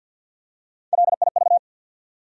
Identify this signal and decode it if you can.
Morse code, heard as a keyed tone, about 52 wpm, 700 Hz. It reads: LIV
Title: USA Common Names - Max 3 Letters - 50wpm